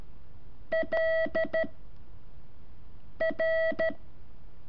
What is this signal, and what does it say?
Morse code, heard as a keyed tone, about 13 wpm, 650 Hz. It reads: L R
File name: curso / ejercicio 4